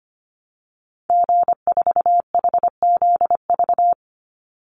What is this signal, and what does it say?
Morse code, heard as a keyed tone, about 25 wpm, 700 Hz. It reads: G4HZV
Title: Call Signs Repeated 3x - 25wpm